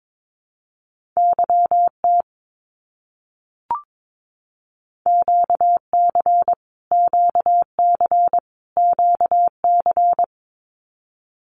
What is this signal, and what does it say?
Morse code, heard as a keyed tone, about 22 wpm, 700 Hz. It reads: YT QC QC QC
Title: Canadian Provinces and Territories - Repeated 3x - 22wpm